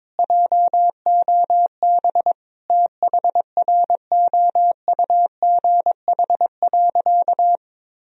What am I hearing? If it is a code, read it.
Morse code, heard as a keyed tone, about 22 wpm, 700 Hz. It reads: JOB THROUGH.
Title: Sentences from Top 1000 Words - Review Part 3 - 22wpm